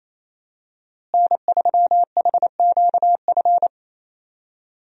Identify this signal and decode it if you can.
Morse code, heard as a keyed tone, about 27 wpm, 700 Hz. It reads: N3HQF
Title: USA Call Signs - 5 Characters - Repeated 2x - 28wpm